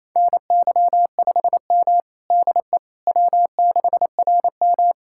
Morse code, heard as a keyed tone, about 28 wpm, 700 Hz. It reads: NY5M DE W6RM
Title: USA Callsign DE Callsign - 4 Characters - 28wpm